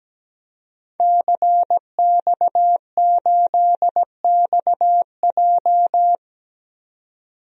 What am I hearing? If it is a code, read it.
Morse code, heard as a keyed tone, about 17 wpm, 700 Hz. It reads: CX8XJ
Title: DXCC-100 Call Signs - 5 Characters - 17wpm